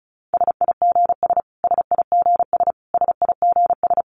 Morse code, heard as a keyed tone, about 34 wpm, 700 Hz. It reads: SIGS SIGS SIGS
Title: Abbreviations Repeated 3x - 35wpm